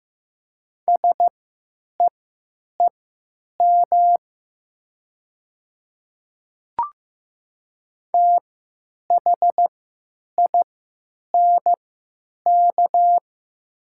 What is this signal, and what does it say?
Morse code, heard as a keyed tone, about 15 wpm, spaced as for 9 wpm, 700 Hz. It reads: SEEM THINK
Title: 3x Character Spacing - Top 300 Words - 15wpm